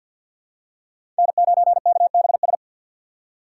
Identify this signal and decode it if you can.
Morse code, heard as a keyed tone, about 52 wpm, 700 Hz. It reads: N9KBS
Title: USA Call Signs - 5 Characters - 6x Character Spacing - 50wpm